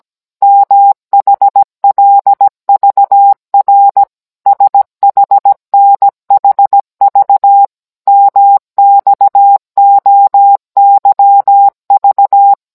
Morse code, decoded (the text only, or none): MHLVR SHNHV MXOYV